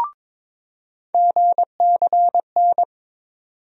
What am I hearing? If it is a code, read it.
Morse code, heard as a keyed tone, about 23 wpm, 700 Hz. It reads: GCN
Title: USA Call Sign Postfixes - 3 Letters 22wpm